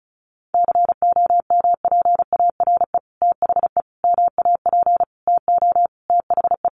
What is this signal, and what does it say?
Morse code, heard as a keyed tone, about 34 wpm, 700 Hz. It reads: COMPARE THE MAP TO THE
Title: Sentences from Top 1300 Words 35wpm